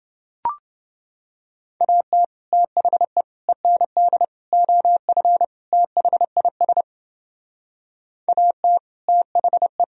Morse code, heard as a keyed tone, about 29 wpm, 700 Hz. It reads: AT THE END OF THIS AT THE